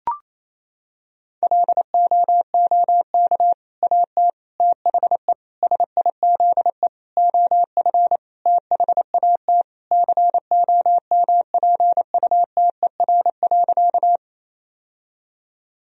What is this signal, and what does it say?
Morse code, heard as a keyed tone, about 28 wpm, 700 Hz. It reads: LOOK AT THE SIZE OF THAT COMPUTER.
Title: Sentences from Top 1000 Words - Review Part 3 - 28wpm